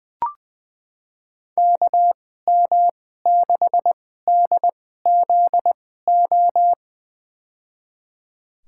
Morse code, heard as a keyed tone, about 20 wpm, spaced as for 14 wpm, 700 Hz. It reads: KM6DZO
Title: USA Call Signs - 6 Characters - 2x Character Spacing - 20wpm